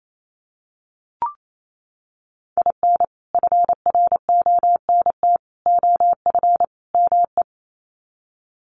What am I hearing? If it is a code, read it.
Morse code, heard as a keyed tone, about 28 wpm, 700 Hz. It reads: IN FRONT OF ME
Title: Sets of 4 Words 28wpm